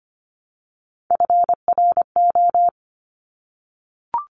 Morse code, heard as a keyed tone, about 25 wpm, 700 Hz. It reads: FRO
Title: Top Three Letter Combinations in Words 25wpm